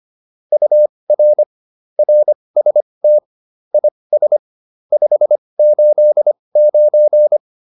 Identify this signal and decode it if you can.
Morse code, heard as a keyed tone, about 25 wpm, spaced as for 20 wpm, 600 Hz. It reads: UR RST IS 589